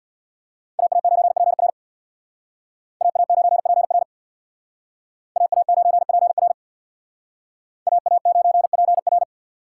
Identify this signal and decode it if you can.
Morse code, heard as a keyed tone, about 62 wpm, 700 Hz. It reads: AA9PR AA9PR AA9PR AA9PR
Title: Speed Racing - USA Call Signs 50wpm